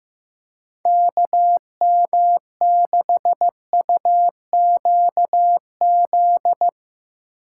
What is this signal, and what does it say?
Morse code, heard as a keyed tone, about 16 wpm, 700 Hz. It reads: KM6UQZ
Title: Speed Racing - USA 2x3 Call Signs 15wpm